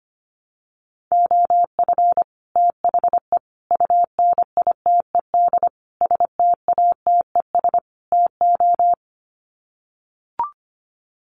Morse code, heard as a keyed tone, about 25 wpm, 700 Hz. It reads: OF THE UNITED STATES TO
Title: Sets of 5 Words Encore 25wpm